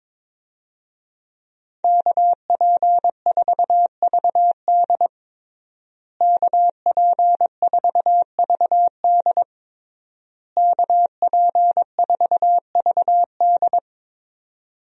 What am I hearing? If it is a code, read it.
Morse code, heard as a keyed tone, about 22 wpm, 700 Hz. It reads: KP4VD KP4VD KP4VD
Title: DXCC-100 Call Signs - 5 Characters - Repeated 3x - 22wpm